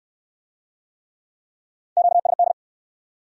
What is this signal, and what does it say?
Morse code, heard as a keyed tone, about 62 wpm, 700 Hz. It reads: QSD